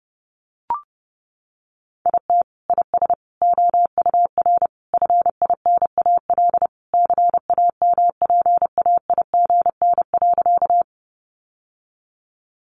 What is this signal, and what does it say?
Morse code, heard as a keyed tone, about 30 wpm, 700 Hz. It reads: IT IS OUR FINAL CAMPAIGN.
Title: Sentences from Top 1200 Words 30wpm